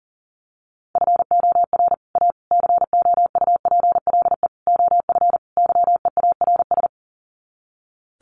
Morse code, heard as a keyed tone, about 40 wpm, 700 Hz. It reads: FOR A COUPLE OF YEARS
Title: Sets of 5 Words Encore 40wpm